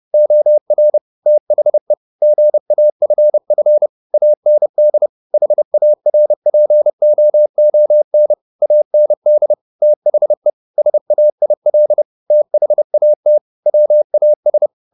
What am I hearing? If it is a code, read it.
Morse code, heard as a keyed tone, about 30 wpm, 600 Hz. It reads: OR THE GAFF AND HARPOON AND THE SAIL THAT WAS
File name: the old man and the sea / the old man and the sea 1 30wpm